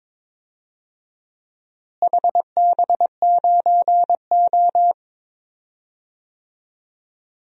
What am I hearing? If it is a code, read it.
Morse code, heard as a keyed tone, about 22 wpm, 700 Hz. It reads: HB9O
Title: DXCC-100 Call Signs - 4 Characters - Repeated 3x - 22wpm